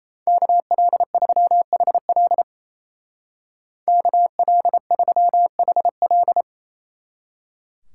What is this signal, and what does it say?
Morse code, heard as a keyed tone, about 31 wpm, 700 Hz. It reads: KL3HL KL3HL
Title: Speed Racing - USA 2x2 Call Signs 28wpm